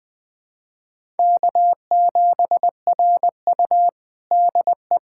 Morse code, heard as a keyed tone, about 20 wpm, 700 Hz. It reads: K7RU DE
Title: USA Callsign DE Callsign - 4 Characters - 20wpm